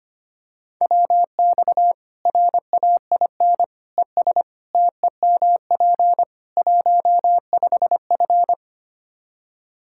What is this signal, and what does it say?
Morse code, heard as a keyed tone, about 25 wpm, 700 Hz. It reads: WX RAIN ES TEMP 15F
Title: Practice QSO - Exchange 2 - Info - 25wpm